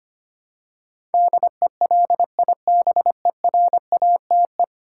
Morse code, heard as a keyed tone, about 25 wpm, 690 Hz. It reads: DELIBERATE
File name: 10 Letter Words - First 3 Letters Given - 25wpm